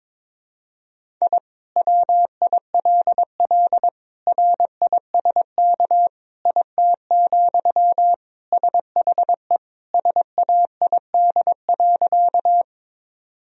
Morse code, heard as a keyed tone, about 22 wpm, 700 Hz. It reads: I WILL RISK IT, SHE SAID.
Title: Sentences from Top 1000 Words - Review Part 4 - 22wpm